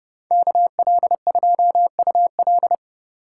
Morse code, heard as a keyed tone, about 30 wpm, 700 Hz. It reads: KL2UL